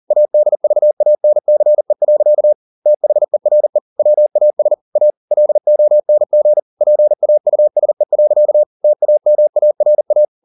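Morse code, heard as a keyed tone, about 40 wpm, 600 Hz. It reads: ADVANCE. THERE WAS A LONG PAUSE. TAMARA
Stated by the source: Morse code at 40 wpm, practice text, CW med Gnister.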